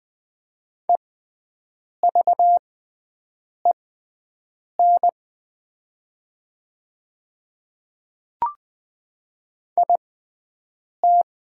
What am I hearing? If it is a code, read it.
Morse code, heard as a keyed tone, about 20 wpm, spaced as for 7 wpm, 700 Hz. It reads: EVEN IT
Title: 6x Character Spacing - Top 200 Words - 20wpm